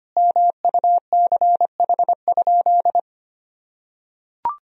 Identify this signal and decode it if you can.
Morse code, heard as a keyed tone, about 25 wpm, 700 Hz. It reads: MUCH?
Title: Sentences from Top 400 Words and Abbreviations 25wpm